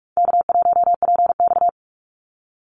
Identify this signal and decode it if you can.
Morse code, heard as a keyed tone, about 44 wpm, 700 Hz. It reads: K1PX
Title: Call Signs - 45wpm